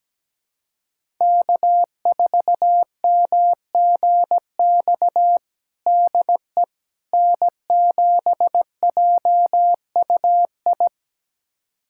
Morse code, heard as a keyed tone, about 17 wpm, 700 Hz. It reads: K4MGX DE N7JUI
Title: USA Callsign DE Callsign - 5 Characters - 17wpm